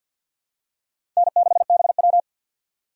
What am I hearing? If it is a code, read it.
Morse code, heard as a keyed tone, about 48 wpm, 700 Hz. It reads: N6BW
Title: USA Call Signs - 4 Characters - 3x Character Spacing - 50wpm